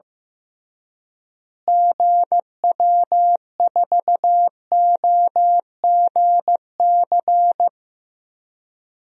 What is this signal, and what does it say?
Morse code, heard as a keyed tone, about 15 wpm, 700 Hz. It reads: GW4OGC
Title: DXCC-100 Call Signs - 6 Characters - Repeated 3x - 15wpm